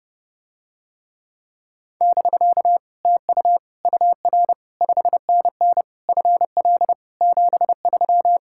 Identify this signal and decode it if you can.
Morse code, heard as a keyed tone, about 32 wpm, 700 Hz. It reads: <BK> TU UR 5NN FL 73